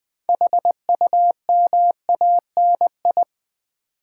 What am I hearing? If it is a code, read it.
Morse code, heard as a keyed tone, about 20 wpm, 700 Hz. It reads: HUMANI
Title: Build A Word - 6 Letters At A Time - 12-Letter Words - 20wpm